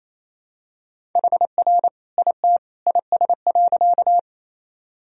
Morse code, heard as a keyed tone, about 28 wpm, 700 Hz. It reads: HR IT IS.
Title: Sentences from Top 200 Words and Abbreviations 28wpm